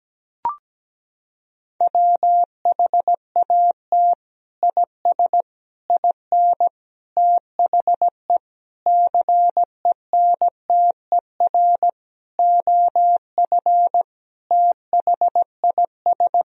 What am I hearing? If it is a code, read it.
Morse code, heard as a keyed tone, about 17 wpm, 700 Hz. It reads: WHAT IS IN THE CENTER OF THIS
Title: Sentences from Top 1000 Words - Review Part 2 - 17wpm